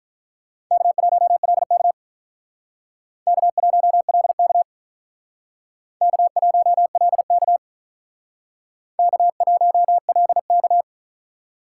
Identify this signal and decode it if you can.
Morse code, heard as a keyed tone, about 45 wpm, 700 Hz. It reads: K1LK K1LK K1LK K1LK